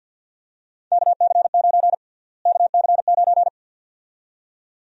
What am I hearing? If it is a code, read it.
Morse code, heard as a keyed tone, about 48 wpm, 700 Hz. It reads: KC9 KC9
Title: USA Call Sign Prefixes - Repeated 2x - 50wpm